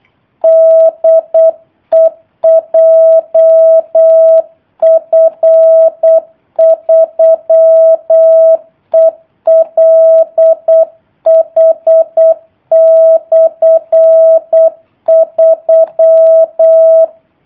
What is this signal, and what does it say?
Morse code, heard as a keyed tone, about 8 wpm, 650 Hz. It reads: DE JF3ELH/3
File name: jh8cbh / 2019cw / sp03- w